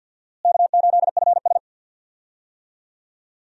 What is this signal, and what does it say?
Morse code, heard as a keyed tone, about 51 wpm, 700 Hz. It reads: K8FS
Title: USA Call Signs - 4 Characters - Repeated 3x - 50wpm